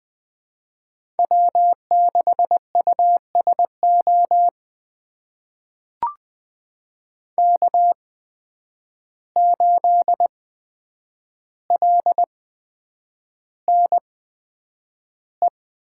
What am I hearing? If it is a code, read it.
Morse code, heard as a keyed tone, about 20 wpm, spaced as for 9 wpm, 700 Hz. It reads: W6USO K8LNE
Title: USA Call Signs - 5 Characters - 8x Character Spacing - 20wpm